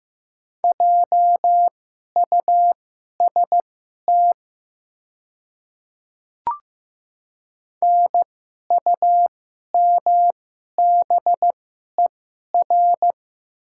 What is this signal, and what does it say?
Morse code, heard as a keyed tone, about 15 wpm, spaced as for 11 wpm, 690 Hz. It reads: JUST NUMBER
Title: 2x Character Spacing - Top 200 Words - 15wpm